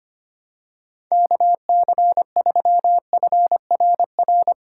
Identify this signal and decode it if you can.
Morse code, heard as a keyed tone, about 25 wpm, 700 Hz. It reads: KC3FRR
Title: USA Call Signs - 2x Character Spacing - 25wpm